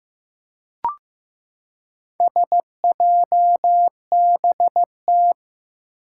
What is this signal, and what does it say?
Morse code, heard as a keyed tone, about 15 wpm, 700 Hz. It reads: SJBT
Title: Four Challenging Letter-Number Combinations 15wpm